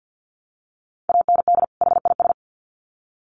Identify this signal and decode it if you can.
Morse code, heard as a keyed tone, about 50 wpm, 700 Hz. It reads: AND HIS
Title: Sets of 2 Words 50wpm